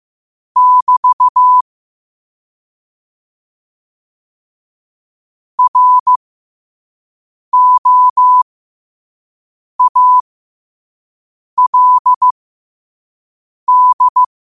Morse code, heard as a keyed tone, about 15 wpm, spaced as for 5 wpm, 1000 Hz. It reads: <BT> ROALD
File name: qotdfiles / 5wpm